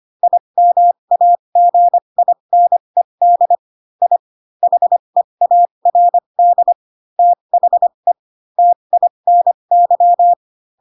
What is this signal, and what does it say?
Morse code, heard as a keyed tone, about 25 wpm, spaced as for 22 wpm, 700 Hz. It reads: IMAGINED I HEARD THE TINY